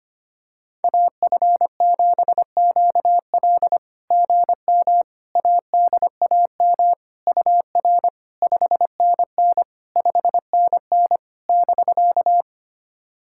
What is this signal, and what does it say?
Morse code, heard as a keyed tone, about 25 wpm, 690 Hz. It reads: AF7QL GM ADAM UR 5NN 5NN <BK>
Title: SOTA QSO Practice 25wpm